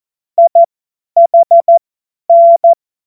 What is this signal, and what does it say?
Morse code, heard as a keyed tone, about 14 wpm, spaced as for 10 wpm, 680 Hz. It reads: IHN
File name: Morsekurs / Witze / Witz 12 März 2024 50 F70